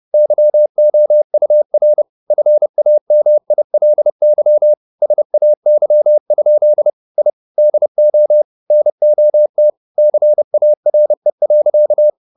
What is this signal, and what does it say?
Morse code, heard as a keyed tone, about 30 wpm, 600 Hz. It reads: YOUR FAMILY SAY? I DO NOT CARE.